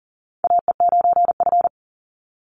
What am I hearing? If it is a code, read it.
Morse code, heard as a keyed tone, about 40 wpm, 700 Hz. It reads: AE9F
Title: NA SOTA Chaser Callsigns 40wpm